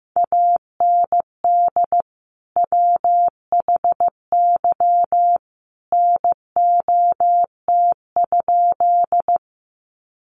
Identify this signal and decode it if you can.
Morse code, heard as a keyed tone, about 15 wpm, 700 Hz. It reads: AND WHY NOT?
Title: Sentences from Top 600 Words 15wpm